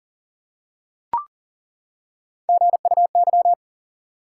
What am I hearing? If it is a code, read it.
Morse code, heard as a keyed tone, about 40 wpm, 700 Hz. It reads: GUY
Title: Three Letter Words - 40wpm